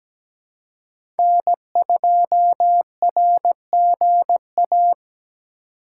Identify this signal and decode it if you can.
Morse code, heard as a keyed tone, about 17 wpm, 700 Hz. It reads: N2RGA
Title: USA Call Signs - 5 Characters - 2x Character Spacing - 17wpm